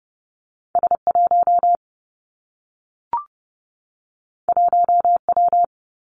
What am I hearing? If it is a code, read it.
Morse code, heard as a keyed tone, about 30 wpm, 700 Hz. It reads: S 1 1 W
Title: Two Challenging Letter-Number Combinations 30wpm